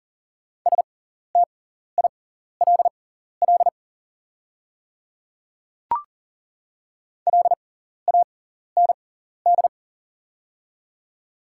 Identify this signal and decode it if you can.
Morse code, heard as a keyed tone, about 40 wpm, spaced as for 14 wpm, 700 Hz. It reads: STILL LAND
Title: 6x Character Spacing - Top 300 Words - 40wpm